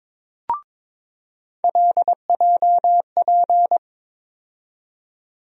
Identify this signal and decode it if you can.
Morse code, heard as a keyed tone, about 22 wpm, 700 Hz. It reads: LJP
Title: USA Call Sign Postfixes - 3 Letters 22wpm